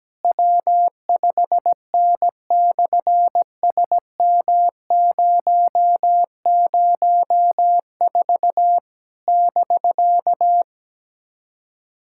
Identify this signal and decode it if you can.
Morse code, heard as a keyed tone, about 17 wpm, 700 Hz. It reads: W5N/SM004 <BK>
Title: SOTA QSO Practice 17wpm